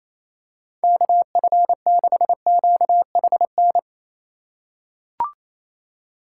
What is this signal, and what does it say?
Morse code, heard as a keyed tone, about 28 wpm, 700 Hz. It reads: KF6QHN
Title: USA Call Signs - 8x Character Spacing - 28wpm